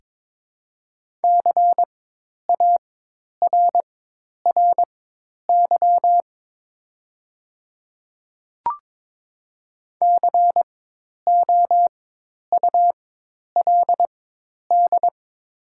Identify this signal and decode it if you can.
Morse code, heard as a keyed tone, about 22 wpm, spaced as for 10 wpm, 700 Hz. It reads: CARRY COULD
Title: 4x Character Spacing - Top 300 Words - 22wpm